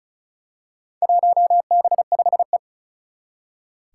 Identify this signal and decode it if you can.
Morse code, heard as a keyed tone, about 36 wpm, 700 Hz. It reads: 1B5E
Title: Four Challenging Letter-Number Combinations 35wpm